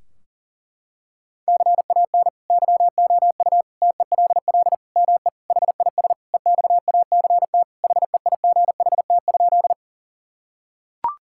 Morse code, heard as a keyed tone, about 40 wpm, 700 Hz. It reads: CAN YOU TELL ME HIS EXACT HEIGHT?
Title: Sentences from Top 1600 Words 40wpm